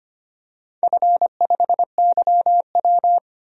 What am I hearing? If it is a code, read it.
Morse code, heard as a keyed tone, about 25 wpm, 700 Hz. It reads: F5YW
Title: DXCC-100 Call Signs - 4 Characters - 25wpm